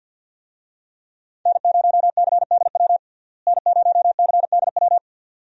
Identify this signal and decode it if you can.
Morse code, heard as a keyed tone, about 48 wpm, 700 Hz. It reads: N0CDW N0CDW
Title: USA Call Signs - 5 Characters - Repeated 2x - 50wpm